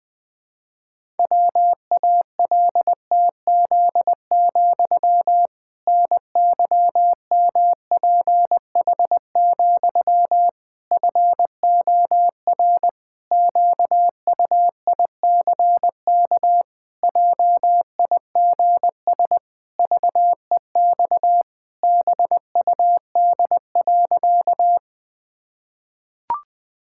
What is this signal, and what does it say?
Morse code, heard as a keyed tone, about 19 wpm, 700 Hz. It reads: WALTZ, NYMPH, FOR QUICK JIGS VEX BUD.
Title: Pangram Challenge v2 - 20wpm